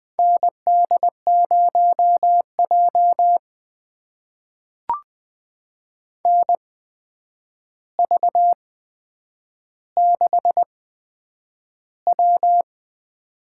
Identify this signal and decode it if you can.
Morse code, heard as a keyed tone, about 20 wpm, spaced as for 9 wpm, 700 Hz. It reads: ND0J NV6W
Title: USA Call Signs - 4 Characters - 8x Character Spacing - 20wpm